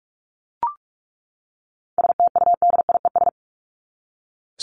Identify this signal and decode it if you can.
Morse code, heard as a keyed tone, about 44 wpm, 700 Hz. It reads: STUDIES
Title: ES:S Suffix - Top 100 Words with Base Words 45wpm